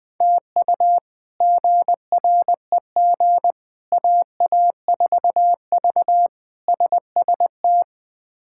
Morse code, heard as a keyed tone, about 21 wpm, 700 Hz. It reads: TU GREG AA4V SST
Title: K1USN SST QSO Practice - 20wpm